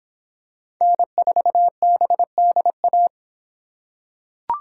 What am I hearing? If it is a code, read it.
Morse code, heard as a keyed tone, about 23 wpm, 700 Hz. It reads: N4BDA